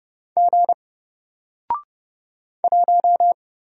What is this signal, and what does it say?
Morse code, heard as a keyed tone, about 31 wpm, 700 Hz. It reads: G 1
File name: Single-Letter-Number-30wpm